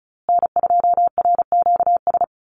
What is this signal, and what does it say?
Morse code, heard as a keyed tone, about 34 wpm, 700 Hz. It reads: N2RQS
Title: USA Call Signs Repeated 2x 35wpm